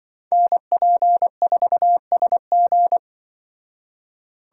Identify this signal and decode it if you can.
Morse code, heard as a keyed tone, about 23 wpm, 700 Hz. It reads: NP4SG
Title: Speed Racing - USA 2x2 Call Signs 20wpm